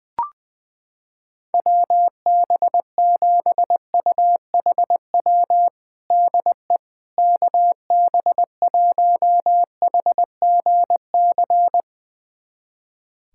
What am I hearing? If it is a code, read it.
Morse code, heard as a keyed tone, about 20 wpm, 700 Hz. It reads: WB7UHW DE KB1HGC
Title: USA Callsign DE Callsign 20wpm